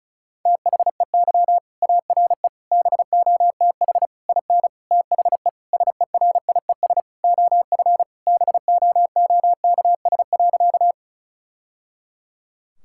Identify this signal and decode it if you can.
Morse code, heard as a keyed tone, about 36 wpm, 700 Hz. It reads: THEY ARE BOTH IN THE SERIES OF BOOKS.
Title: Sentences from Top 1000 Words - Review Part 5 - 35wpm